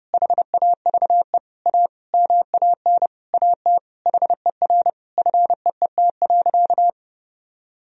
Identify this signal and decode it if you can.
Morse code, heard as a keyed tone, about 30 wpm, 700 Hz. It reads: HAVE A MAN AT HER FEET.
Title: Sentences from Top 1000 Words - Review Part 1 - 30wpm